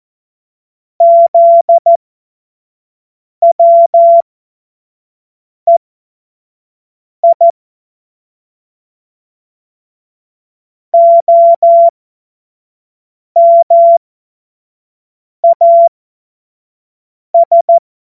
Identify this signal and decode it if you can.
Morse code, heard as a keyed tone, about 14 wpm, spaced as for 5 wpm, 680 Hz. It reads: ZWEI OMAS
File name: Morsekurs / Witze / Witz 05. Oktober 2021, 25 F70